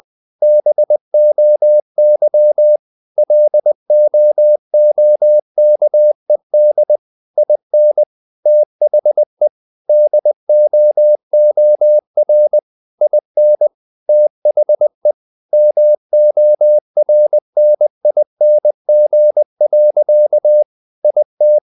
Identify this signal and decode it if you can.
Morse code, heard as a keyed tone, about 20 wpm, 600 Hz. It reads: BOY LOOKED IN THE DOOR IN THE MORNING. IT
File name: the old man and the sea part 2 20wpm